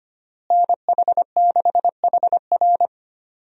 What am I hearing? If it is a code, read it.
Morse code, heard as a keyed tone, about 25 wpm, 700 Hz. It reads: NH6HR
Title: USA Call Signs - 5 Characters - Repeated 3x - 25wpm